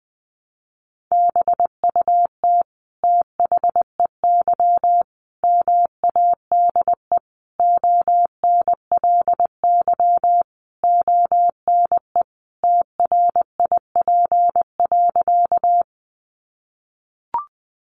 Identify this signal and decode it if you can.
Morse code, heard as a keyed tone, about 20 wpm, 700 Hz. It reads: BUT THEY MADE ONLY ONE TRIP.
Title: Sentences from Top 1300 Words 20wpm